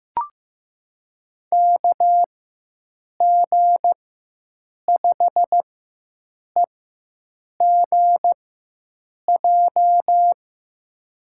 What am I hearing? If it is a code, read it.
Morse code, heard as a keyed tone, about 15 wpm, spaced as for 7 wpm, 700 Hz. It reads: KG5EGJ
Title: USA Call Signs - 6 Characters - 4x Character Spacing - 15wpm